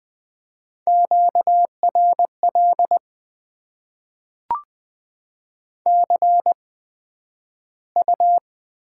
Morse code, heard as a keyed tone, about 20 wpm, spaced as for 11 wpm, 700 Hz. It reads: QRL CU
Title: Top 20 QSO Elements - 8x Character Spacing - 20wpm